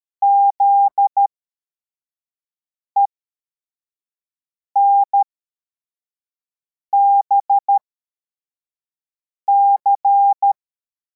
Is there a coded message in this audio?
Yes: Morse code from 0.2 to 10.5 s, heard as a keyed tone, about 13 wpm, spaced as for 4 wpm, 800 Hz. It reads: ZENBC